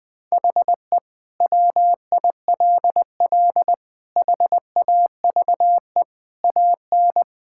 Morse code, heard as a keyed tone, about 20 wpm, 700 Hz. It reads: HE WILL HAVE AN
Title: Sentences from Top 1000 Words - Review Part 5 - 20wpm